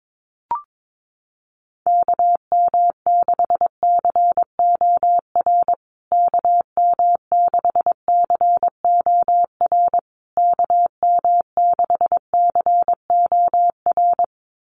Morse code, heard as a keyed tone, about 22 wpm, 700 Hz. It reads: KM6COR KM6COR KM6COR
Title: USA Call Signs Repeated 3x 22wpm